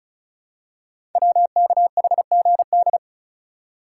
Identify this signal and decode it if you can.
Morse code, heard as a keyed tone, about 36 wpm, 700 Hz. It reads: WKHGD
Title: Code Groups - 5 Letters - 35wpm